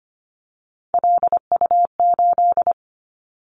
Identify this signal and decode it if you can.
Morse code, heard as a keyed tone, about 25 wpm, 700 Hz. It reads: LU8
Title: Callsign Prefixes for Top 100 DXCC - 25wpm